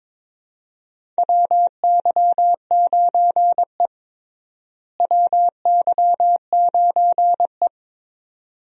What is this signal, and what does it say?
Morse code, heard as a keyed tone, about 22 wpm, 700 Hz. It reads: WY9E WY9E
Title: USA Call Signs - 4 Characters - Repeated 2x - 22wpm